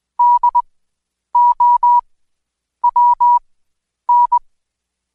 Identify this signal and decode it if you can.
Morse code, heard as a keyed tone, about 20 wpm, spaced as for 9 wpm, 970 Hz. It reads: DOWN
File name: Code Quick / AudioDownld2 / Seg115-Day20-DayDrill 20